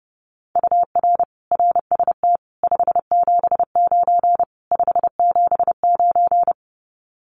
Morse code, heard as a keyed tone, about 30 wpm, 700 Hz. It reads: UR RST 579 579
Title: Common QSO Phrases Encore 30wpm